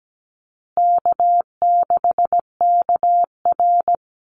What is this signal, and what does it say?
Morse code, heard as a keyed tone, about 17 wpm, 700 Hz. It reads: K6KR
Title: Easy Call Signs Repeated 3x - 17wpm